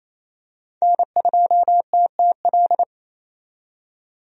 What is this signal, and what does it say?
Morse code, heard as a keyed tone, about 28 wpm, 700 Hz. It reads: N2TTL
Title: USA Call Signs - 5 Characters - 28wpm